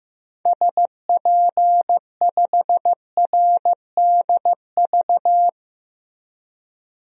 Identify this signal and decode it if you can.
Morse code, heard as a keyed tone, about 15 wpm, 700 Hz. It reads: SP5RDV
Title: DXCC-100 Call Signs - 6 Characters - Repeated 3x - 15wpm